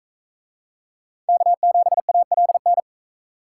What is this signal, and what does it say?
Morse code, heard as a keyed tone, about 42 wpm, 700 Hz. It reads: K7ALN